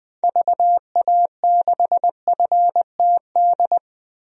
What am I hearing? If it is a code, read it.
Morse code, heard as a keyed tone, about 20 wpm, 700 Hz. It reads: VA6FTD
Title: DXCC-100 Call Signs - 6 Characters - Repeated 3x - 20wpm